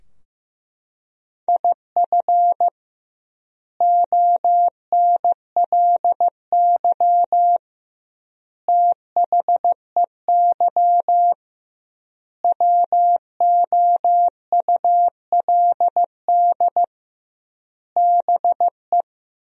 Morse code, heard as a keyed tone, about 15 wpm, 700 Hz. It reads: IF ONLY THEY WOULD BE
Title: 2x Word-Spacing - Sentences from Top 500 Words - 15wpm